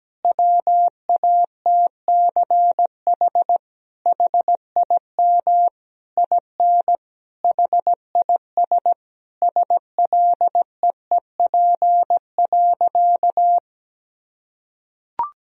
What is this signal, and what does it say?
Morse code, heard as a keyed tone, about 17 wpm, 700 Hz. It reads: WATCH HIM IN HIS SLEEP.
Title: Sentences from Top 1000 Words - Review Part 1 - 17wpm